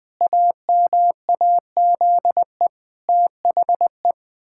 Morse code, heard as a keyed tone, about 19 wpm, 700 Hz. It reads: AMAZE THE
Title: Pangram Challenge v2 - 20wpm